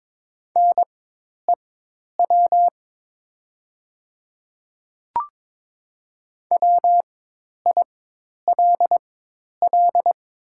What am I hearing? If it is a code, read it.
Morse code, heard as a keyed tone, about 22 wpm, spaced as for 10 wpm, 700 Hz. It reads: NEW WILL
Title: 4x Character Spacing - Top 200 Words - 22wpm